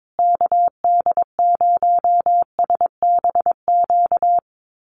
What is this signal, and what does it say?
Morse code, heard as a keyed tone, about 22 wpm, 700 Hz. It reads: KD0SBQ
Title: USA Call Signs Repeated 3x 22wpm